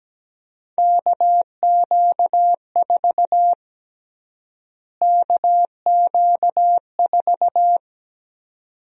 Morse code, heard as a keyed tone, about 17 wpm, 700 Hz. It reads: KQ4 KQ4
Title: USA Call Sign Prefixes - Repeated 2x - 17wpm